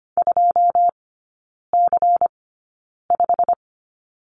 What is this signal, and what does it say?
Morse code, heard as a keyed tone, about 25 wpm, spaced as for 9 wpm, 700 Hz. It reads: 2C5
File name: course / 25:10wpm / 093 Lesson 16e - Letter c - Three Characters 25wpm